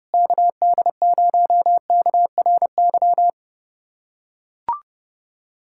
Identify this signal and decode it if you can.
Morse code, heard as a keyed tone, about 30 wpm, 700 Hz. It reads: KD0KRY
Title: USA Callsign DE Callsign - 6 Characters - 30wpm